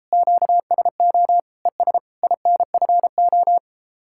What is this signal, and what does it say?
Morse code, heard as a keyed tone, about 37 wpm, 700 Hz. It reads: QSO ES INFO